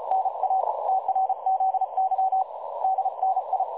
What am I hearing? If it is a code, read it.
Morse code, heard as a keyed tone, about 34 wpm, 760 Hz. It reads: NA TRZY DNI
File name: dlazaawansowanych / nagraniazpasma / websdr recording 3565.0kHz